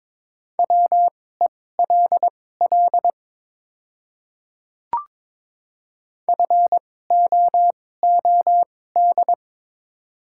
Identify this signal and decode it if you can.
Morse code, heard as a keyed tone, about 21 wpm, spaced as for 16 wpm, 700 Hz. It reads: WELL FOOD
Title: 2x Character Spacing - Top 300 Words - 22wpm